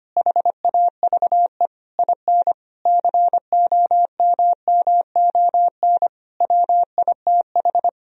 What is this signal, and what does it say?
Morse code, heard as a keyed tone, about 25 wpm, 700 Hz. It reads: HAVE IN COMMON WITH
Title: Sentences from Top 1000 Words - Review Part 3 - 25wpm